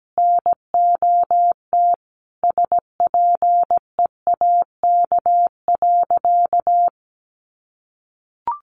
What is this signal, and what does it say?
Morse code, heard as a keyed tone, about 17 wpm, 700 Hz. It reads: NOT SPEAK.
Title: Sentences from Top 1100 Words 17wpm